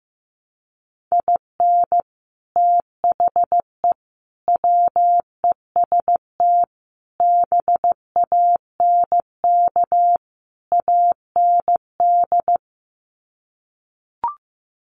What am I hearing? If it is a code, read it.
Morse code, heard as a keyed tone, about 15 wpm, 700 Hz. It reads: IN THE WEST BANK AND
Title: Sets of 5 Words Encore2 15wpm